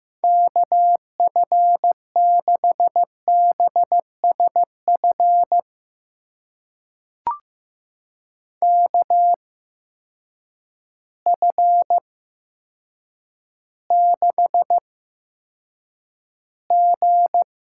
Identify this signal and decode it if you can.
Morse code, heard as a keyed tone, about 15 wpm, spaced as for 8 wpm, 700 Hz. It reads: KF6BSF KF6G
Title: USA Call Signs - 6 Characters - 8x Character Spacing - 15wpm